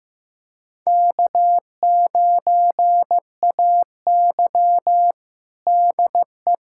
Morse code, heard as a keyed tone, about 15 wpm, 700 Hz. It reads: K9AY DE
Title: USA Callsign DE Callsign 15wpm